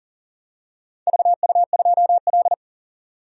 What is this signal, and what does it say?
Morse code, heard as a keyed tone, about 37 wpm, 700 Hz. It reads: VU2L
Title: DXCC-100 Call Signs - 4 Characters - 40wpm